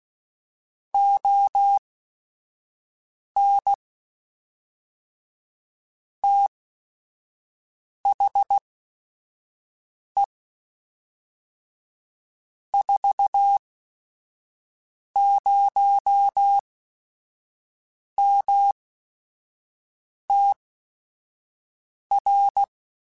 Morse code, heard as a keyed tone, about 16 wpm, spaced as for 5 wpm, 800 Hz. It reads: ON THE 40MTR
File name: QSOs / QSOs-mp3s / kc7sac@5